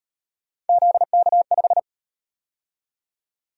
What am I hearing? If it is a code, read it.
Morse code, heard as a keyed tone, about 37 wpm, 700 Hz. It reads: ZK5